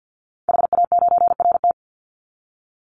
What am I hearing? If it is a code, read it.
Morse code, heard as a keyed tone, about 50 wpm, 700 Hz. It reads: HA9RT
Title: Call Signs - 50wpm